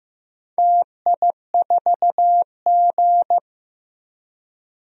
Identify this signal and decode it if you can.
Morse code, heard as a keyed tone, about 15 wpm, 700 Hz. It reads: TI4G
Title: DXCC-100 Call Signs - 4 Characters - Repeated 2x - 15wpm